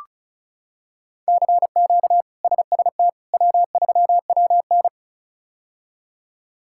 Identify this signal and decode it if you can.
Morse code, heard as a keyed tone, about 36 wpm, 700 Hz. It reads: CQ SST W3WN
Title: K1USN SST QSO Practice - 35wpm